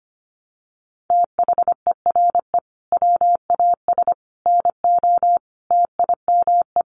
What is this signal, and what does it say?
Morse code, heard as a keyed tone, about 25 wpm, 700 Hz. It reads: THERE WAS NO TIME
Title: Sentences from Top 1100 Words 25wpm